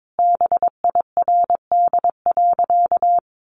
Morse code, heard as a keyed tone, about 22 wpm, 700 Hz. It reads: BIRD.
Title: Sentences from Top 1400 Words 22wpm